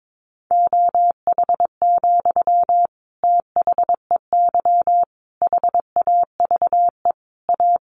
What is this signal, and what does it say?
Morse code, heard as a keyed tone, about 22 wpm, 700 Hz. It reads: OH, THEY HAVE A
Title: Sentences from Top 800 Words 22wpm